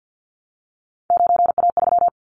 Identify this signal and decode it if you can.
Morse code, heard as a keyed tone, about 50 wpm, 700 Hz. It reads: 9A3
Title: Callsign Prefixes for Top 100 DXCC - 50wpm